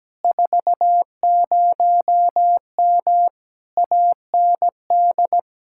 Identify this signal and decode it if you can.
Morse code, heard as a keyed tone, about 17 wpm, 700 Hz. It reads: 40M AND
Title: Speed-Racing Common QSO Phrases - 17wpm